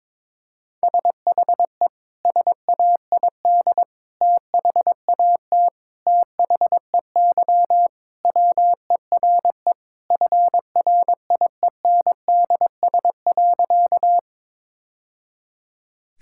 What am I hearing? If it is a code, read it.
Morse code, heard as a keyed tone, about 22 wpm, 700 Hz. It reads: SHE SAID THAT THEY WERE FRIENDS.
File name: practice / 22wpm / Sentences with 3-Word Phrases - 22wpm